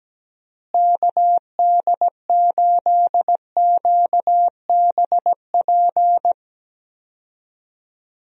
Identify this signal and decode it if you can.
Morse code, heard as a keyed tone, about 17 wpm, 700 Hz. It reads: KD8QBP
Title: USA Call Signs - 6 Characters - Repeated 3x - 17wpm